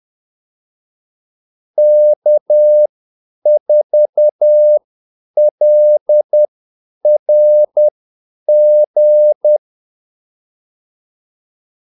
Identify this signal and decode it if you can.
Morse code, heard as a keyed tone, about 10 wpm, spaced as for 8 wpm, 600 Hz. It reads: K4LRG
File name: Callsigns-10-8wpm